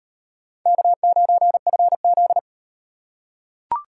Morse code, heard as a keyed tone, about 35 wpm, 700 Hz. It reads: K9FZ